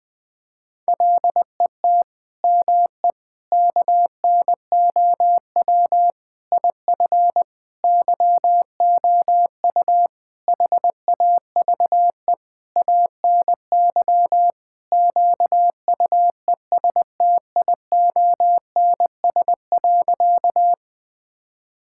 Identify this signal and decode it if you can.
Morse code, heard as a keyed tone, about 20 wpm, 700 Hz. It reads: LET ME KNOW IF YOU HAVE ANY QUESTIONS.
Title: Sentences with 7-Word Phrases - 20wpm